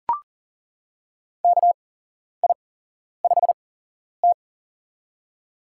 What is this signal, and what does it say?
Morse code, heard as a keyed tone, about 40 wpm, spaced as for 11 wpm, 700 Hz. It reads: KI5T
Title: USA Call Signs - 8x Character Spacing - 40wpm